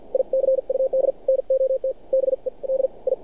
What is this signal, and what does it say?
Morse code, heard as a keyed tone, about 53 wpm, spaced as for 46 wpm, 530 Hz. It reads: I CUD NOT BE D I
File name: archived / CRSnet